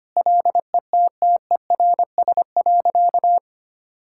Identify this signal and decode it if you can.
Morse code, heard as a keyed tone, about 25 wpm, 700 Hz. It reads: LETTERS.
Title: Sentences from Top 1000 Words - Review Part 6 - 25wpm